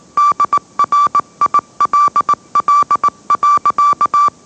Morse code, heard as a keyed tone, about 18 wpm, 1160 Hz. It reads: DRILL.
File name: mrx / mrx081 20wpm